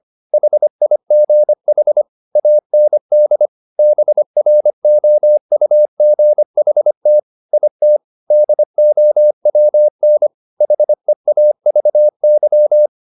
Morse code, heard as a keyed tone, about 25 wpm, 600 Hz. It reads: HIGH AND BROUGHT IT DOWN HEAVY